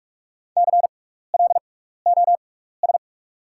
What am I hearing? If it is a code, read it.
Morse code, heard as a keyed tone, about 48 wpm, spaced as for 16 wpm, 700 Hz. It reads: CLOS
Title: 6x Character Spacing - Top 300 Words - 45wpm